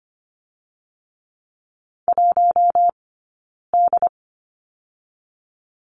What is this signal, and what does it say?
Morse code, heard as a keyed tone, about 25 wpm, 700 Hz. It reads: 1 D
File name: course / 25:10wpm / 092 Lesson 16d - Letter c - Two Characters 25wpm